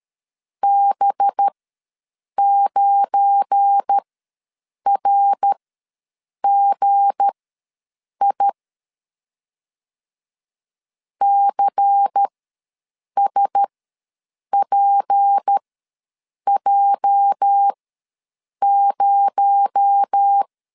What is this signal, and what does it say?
Morse code, heard as a keyed tone, about 13 wpm, spaced as for 7 wpm, 800 Hz. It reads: B9RGI CSPJ0
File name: LEZIONI / 8 YJ45 / 8 YJ ws 12wpm 3342